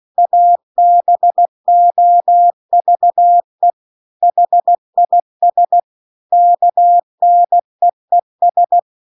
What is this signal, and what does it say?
Morse code, heard as a keyed tone, about 16 wpm, 700 Hz. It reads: ABOVE HIS KNEES
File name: CW-The-Big-Twoheaded-River-2-16WPM